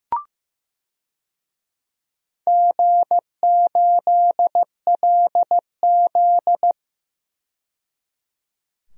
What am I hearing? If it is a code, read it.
Morse code, heard as a keyed tone, about 15 wpm, 700 Hz. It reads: G8LZ
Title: DXCC-100 Call Signs - 4 Characters - 15wpm